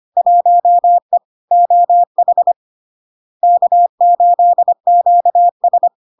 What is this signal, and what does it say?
Morse code, heard as a keyed tone, about 25 wpm, 700 Hz. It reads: 1E OH K8QS
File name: Log-in-CW-25wpm-700-Hz